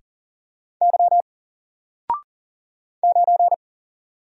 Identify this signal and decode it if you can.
Morse code, heard as a keyed tone, about 38 wpm, 700 Hz. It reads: Y 9
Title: Single Letter-Number 40wpm